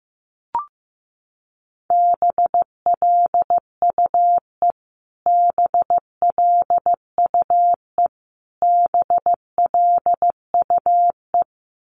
Morse code, heard as a keyed tone, about 15 wpm, 700 Hz. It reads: BLUE BLUE BLUE
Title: Top 500 Words Repeated 15wpm